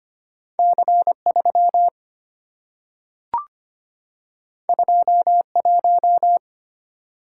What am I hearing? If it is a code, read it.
Morse code, heard as a keyed tone, about 25 wpm, 700 Hz. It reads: C3 21
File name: Two Letter-Number Combinations - 25wpm